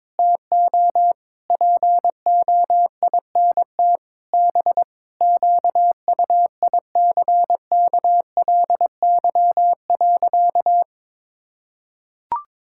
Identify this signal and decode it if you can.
Morse code, heard as a keyed tone, about 22 wpm, 700 Hz. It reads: TO POINT B QUICKLY.
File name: Sentences with 7-Word Phrases - 22wpm